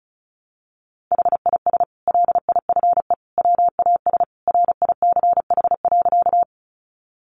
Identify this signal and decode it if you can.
Morse code, heard as a keyed tone, about 34 wpm, 700 Hz. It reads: HIS LIFE WAS RICH.
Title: Sentences from Top 1100 Words 35wpm